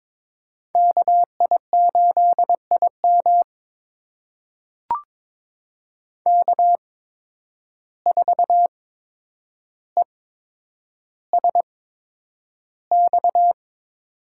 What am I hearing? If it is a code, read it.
Morse code, heard as a keyed tone, about 22 wpm, spaced as for 9 wpm, 700 Hz. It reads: KI8IM K4ESX
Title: USA Call Signs - 5 Characters - 8x Character Spacing - 22wpm